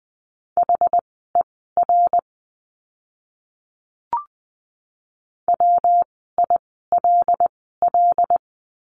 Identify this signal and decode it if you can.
Morse code, heard as a keyed tone, about 20 wpm, spaced as for 14 wpm, 700 Hz. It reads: HER WILL
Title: 237 Lesson 38f - Letter q - Words 20wpm